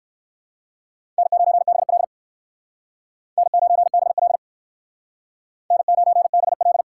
Morse code, heard as a keyed tone, about 59 wpm, 700 Hz. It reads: N9BL N9BL N9BL